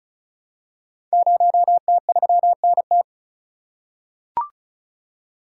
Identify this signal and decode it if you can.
Morse code, heard as a keyed tone, about 36 wpm, 700 Hz. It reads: 0T3NT
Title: Code Groups - 5 Letter:Numbers - 35wpm